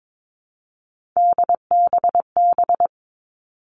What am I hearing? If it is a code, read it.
Morse code, heard as a keyed tone, about 22 wpm, 700 Hz. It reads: DBB
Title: All Cut Numbers - 3-Digit 22wpm